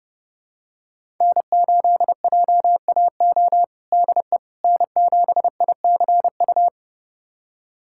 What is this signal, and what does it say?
Morse code, heard as a keyed tone, about 29 wpm, 700 Hz. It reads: N8JAO DE N7ICU
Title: USA Callsign DE Callsign 30wpm